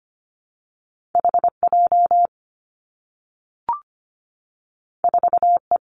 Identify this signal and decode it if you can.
Morse code, heard as a keyed tone, about 25 wpm, 700 Hz. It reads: HJ 4E
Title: Two Challenging Letter-Number Combinations 25wpm